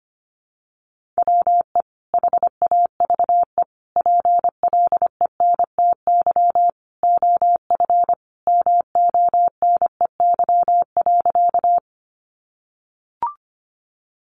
Morse code, heard as a keyed tone, about 25 wpm, 700 Hz. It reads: WE HAVE PLENTY OF MONEY.
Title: Sentences from Top 1300 Words 25wpm